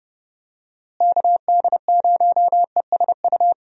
Morse code, heard as a keyed tone, about 31 wpm, 700 Hz. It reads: KD0ESU
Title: USA Call Signs - 6 Characters - Repeated 3x - 30wpm